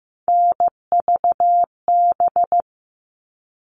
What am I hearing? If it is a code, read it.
Morse code, heard as a keyed tone, about 15 wpm, 700 Hz. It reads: NVB
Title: All Cut Numbers - 3-Digit 15wpm